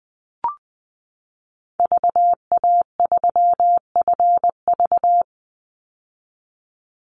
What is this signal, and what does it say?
Morse code, heard as a keyed tone, about 20 wpm, 700 Hz. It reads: VA3FV
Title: Call Signs - 20wpm